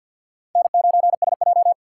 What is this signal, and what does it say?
Morse code, heard as a keyed tone, about 50 wpm, 700 Hz. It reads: N9SJ
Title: USA Call Signs - 3x Character Spacing - 50wpm